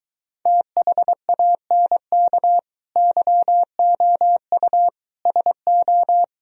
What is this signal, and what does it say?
Morse code, heard as a keyed tone, about 24 wpm, 700 Hz. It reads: THANK YOU SO